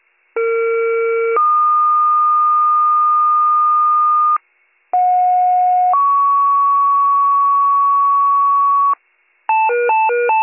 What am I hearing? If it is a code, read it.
Two-tone page, heard as a one-second tone, then a three-second tone, about 460 then 1220 Hz, about 730 then 1120 Hz. It is heard from 0.4 to 8.9 s.